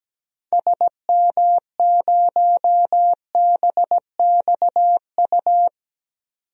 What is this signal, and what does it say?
Morse code, heard as a keyed tone, about 17 wpm, 700 Hz. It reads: SM0BXU
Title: DXCC-100 Call Signs - 6 Characters - 17wpm